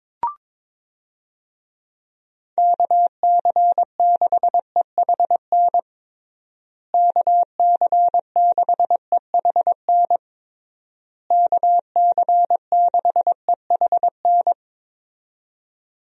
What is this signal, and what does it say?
Morse code, heard as a keyed tone, about 22 wpm, 700 Hz. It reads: KC6EHN KC6EHN KC6EHN
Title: USA Call Signs - 6 Characters - Repeated 3x - 22wpm